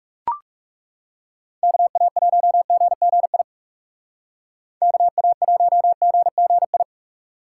Morse code, heard as a keyed tone, about 39 wpm, 700 Hz. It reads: KA1GGI KA1GGI